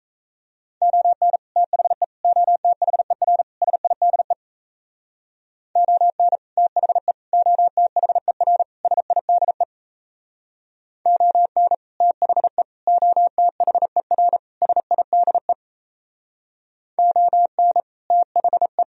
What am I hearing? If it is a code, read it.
Morse code, heard as a keyed tone, about 36 wpm, 700 Hz. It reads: ON THE OTHER SIDE ON THE OTHER SIDE ON THE OTHER SIDE ON THE